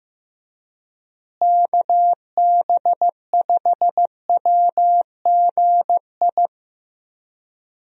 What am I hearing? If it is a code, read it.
Morse code, heard as a keyed tone, about 15 wpm, 700 Hz. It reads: KB5WGI
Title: USA Call Signs - 6 Characters - Repeated 3x - 15wpm